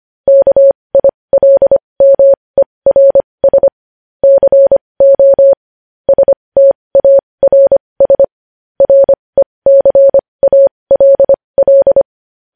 Morse code, heard as a keyed tone, about 25 wpm, spaced as for 20 wpm, 550 Hz. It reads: KILMERS CO STARS RECALL
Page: News headlines in morse code at 20 wpm